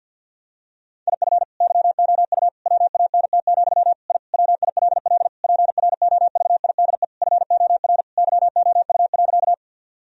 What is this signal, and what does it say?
Morse code, heard as a keyed tone, about 50 wpm, 700 Hz. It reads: IF YOU WANT, I WILL PROVIDE FOR YOU.
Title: Sentences from Top 1000 Words - Review Part 3 - 50wpm